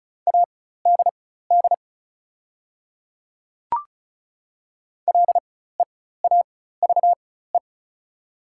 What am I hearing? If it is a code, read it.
Morse code, heard as a keyed tone, about 36 wpm, spaced as for 16 wpm, 690 Hz. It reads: ADD LEAVE
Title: 4x Character Spacing - Top 500 Words - 35wpm